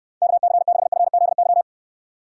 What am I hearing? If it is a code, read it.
Morse code, heard as a keyed tone, about 69 wpm, 700 Hz. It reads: KC6FZY